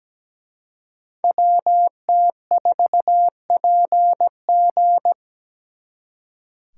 Morse code, heard as a keyed tone, about 20 wpm, 700 Hz. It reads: WT4PG
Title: Speed Racing - USA 2x2 Call Signs 17wpm